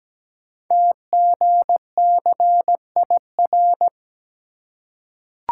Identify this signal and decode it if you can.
Morse code, heard as a keyed tone, about 17 wpm, 700 Hz. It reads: TGCIR
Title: Code Groups - 5 Letters - 17wpm